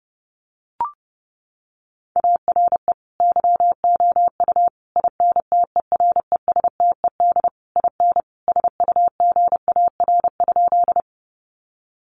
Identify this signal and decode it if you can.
Morse code, heard as a keyed tone, about 30 wpm, 700 Hz. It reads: ARE YOU INTERESTED IN SUGAR?
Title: Sentences from Top 1200 Words 30wpm